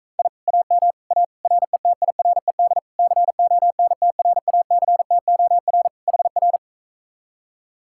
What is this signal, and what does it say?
Morse code, heard as a keyed tone, about 45 wpm, 700 Hz. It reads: I AM A RETIRED CONTRACTOR HR